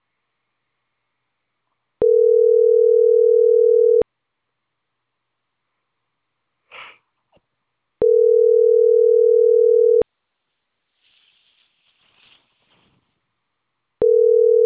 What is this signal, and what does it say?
Ringback tone (North America), 440 + 480 Hz.